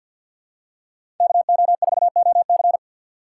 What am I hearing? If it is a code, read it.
Morse code, heard as a keyed tone, about 47 wpm, 700 Hz. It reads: KO4OC
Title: USA Call Signs - 6x Character Spacing - 50wpm